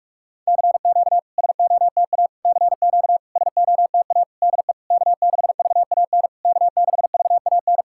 Morse code, heard as a keyed tone, about 47 wpm, 700 Hz. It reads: CQ SOTA CQ SOTA DE K6VAN K6VAN
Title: SOTA QSO Practice 45wpm